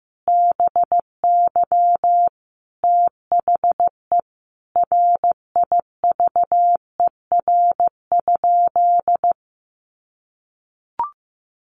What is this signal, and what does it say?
Morse code, heard as a keyed tone, about 15 wpm, 700 Hz. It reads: BY THE RIVER?
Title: Sentences from Top 1400 Words 15wpm